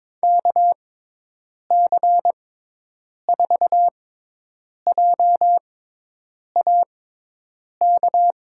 Morse code, heard as a keyed tone, about 22 wpm, spaced as for 8 wpm, 700 Hz. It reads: KC4JAK
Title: USA Call Signs - 6 Characters - 6x Character Spacing - 22wpm